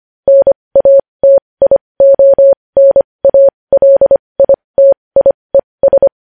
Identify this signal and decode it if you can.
Morse code, heard as a keyed tone, about 25 wpm, spaced as for 20 wpm, 550 Hz. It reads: NATIONALITIES
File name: podcast / News headlines-1x-25x20